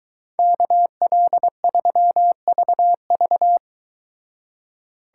Morse code, heard as a keyed tone, about 22 wpm, 700 Hz. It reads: KL3VV